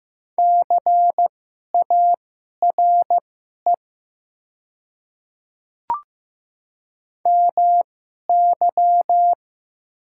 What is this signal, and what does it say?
Morse code, heard as a keyed tone, about 15 wpm, spaced as for 11 wpm, 700 Hz. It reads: CARE MY
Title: 2x Character Spacing - Top 500 Words - 15wpm